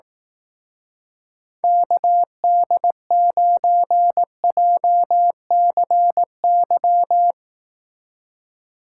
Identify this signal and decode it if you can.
Morse code, heard as a keyed tone, about 18 wpm, 700 Hz. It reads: KD9JCY
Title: Speed Racing - USA 2x3 Call Signs 15wpm